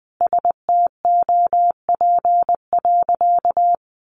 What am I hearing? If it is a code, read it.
Morse code, heard as a keyed tone, about 20 wpm, 700 Hz. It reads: STOP.
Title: Sentences from Top 1500 Words 20wpm